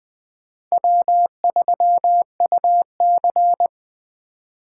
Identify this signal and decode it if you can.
Morse code, heard as a keyed tone, about 20 wpm, 700 Hz. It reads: W3UC
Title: Speed Racing - USA Call Signs 17wpm